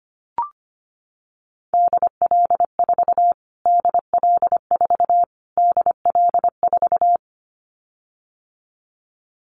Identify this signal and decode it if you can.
Morse code, heard as a keyed tone, about 25 wpm, 700 Hz. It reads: DL4 DL4 DL4
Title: Callsign Prefixes for Top 25 DXCC - Repeated 3x - 25wpm